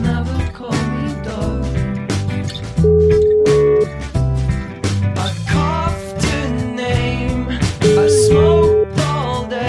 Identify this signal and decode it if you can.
Ringback tone (Europe, 425 Hz), with noise.